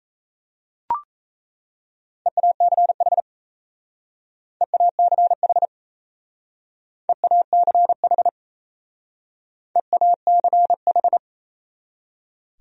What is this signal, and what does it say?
Morse code, heard as a keyed tone, about 36 wpm, 700 Hz. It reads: EACH EACH EACH EACH